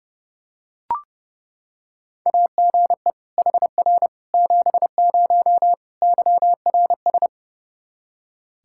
Morse code, heard as a keyed tone, about 30 wpm, 700 Hz. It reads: AGE HR 70 YRS
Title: Practice QSO - Exchange 3 - More Info - 30wpm